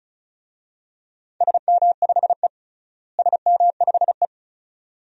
Morse code, heard as a keyed tone, about 36 wpm, 700 Hz. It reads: SM5E SM5E
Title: DXCC-100 Call Signs - 4 Characters - Repeated 2x - 35wpm